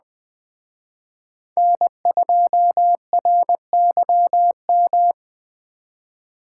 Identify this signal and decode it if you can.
Morse code, heard as a keyed tone, about 20 wpm, 700 Hz. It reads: N2RYM
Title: Speed Racing - USA 1x3 Call Signs 17wpm